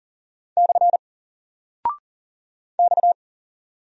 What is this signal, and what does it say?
Morse code, heard as a keyed tone, about 40 wpm, 700 Hz. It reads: / X
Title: Single Character (Letter:Number:Punctuation:Prosign) 40wpm